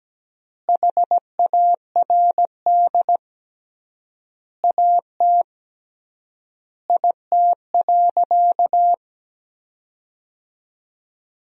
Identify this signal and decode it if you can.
Morse code, heard as a keyed tone, about 17 wpm, 700 Hz. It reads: HARD AT IT.
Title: 3x Word-Spacing - Sentences from Top 300 Words - 17wpm